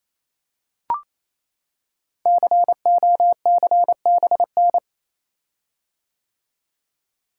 Morse code, heard as a keyed tone, about 27 wpm, 700 Hz. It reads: COCBN
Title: Code Groups - 5 Letters - 28wpm